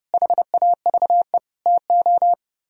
Morse code, heard as a keyed tone, about 37 wpm, 700 Hz. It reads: HAVE TO
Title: Speed-Racing - Sets of 3 Words 30wpm